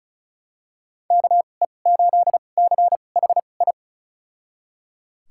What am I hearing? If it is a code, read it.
Morse code, heard as a keyed tone, about 36 wpm, spaced as for 25 wpm, 700 Hz. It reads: KE8CHI
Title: USA Call Signs - 6 Characters - 2x Character Spacing - 35wpm